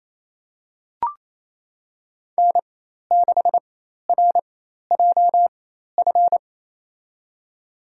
Morse code, heard as a keyed tone, about 28 wpm, spaced as for 13 wpm, 700 Hz. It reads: N6RJF
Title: USA Call Signs - 5 Characters - 4x Character Spacing - 28wpm